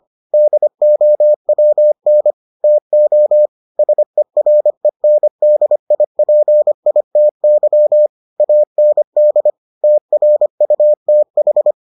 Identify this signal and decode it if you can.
Morse code, heard as a keyed tone, about 25 wpm, 600 Hz. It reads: DOWN TO SERENDIPITY AND TRUTH